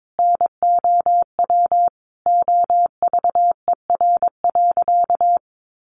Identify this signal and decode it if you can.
Morse code, heard as a keyed tone, about 22 wpm, 700 Hz. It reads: NOW OVER.
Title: Sentences from Top 1100 Words 22wpm